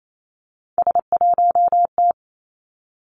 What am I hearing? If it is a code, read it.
Morse code, heard as a keyed tone, about 28 wpm, 700 Hz. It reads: S1T
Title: Three Challenging Letter-Number Combinations 28wpm